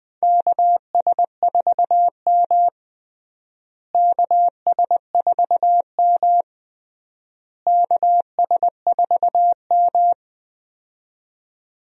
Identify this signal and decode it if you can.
Morse code, heard as a keyed tone, about 19 wpm, 700 Hz. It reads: KS4M KS4M KS4M
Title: USA Call Signs - 4 Characters - Repeated 3x - 20wpm